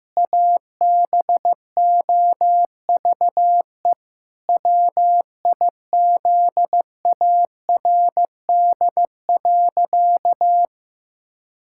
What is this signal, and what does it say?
Morse code, heard as a keyed tone, about 15 wpm, 700 Hz. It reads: ABOVE WIZARD.
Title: Pangram Challenge v2 - 15wpm